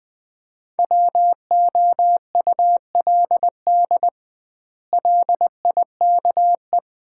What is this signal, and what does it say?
Morse code, heard as a keyed tone, about 21 wpm, 700 Hz. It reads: WOULD LIKE
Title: 2x Word-Spacing - Sets of 3 Words - 20wpm